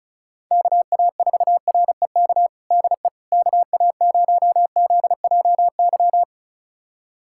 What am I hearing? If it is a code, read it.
Morse code, heard as a keyed tone, about 36 wpm, 700 Hz. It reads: KA4REK DE KA0ZJY
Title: USA Callsign DE Callsign 35wpm